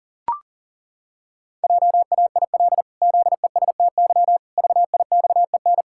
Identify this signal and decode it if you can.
Morse code, heard as a keyed tone, about 40 wpm, 700 Hz. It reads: JAIL ZESTY VIXEN
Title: Pangram Challenge v2 - 40wpm